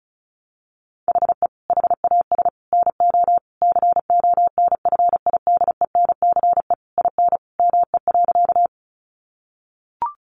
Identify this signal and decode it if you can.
Morse code, heard as a keyed tone, about 34 wpm, 700 Hz. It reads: HE HAS NO CONFIDENCE IN ME.
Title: Sentences from Top 1500 Words 35wpm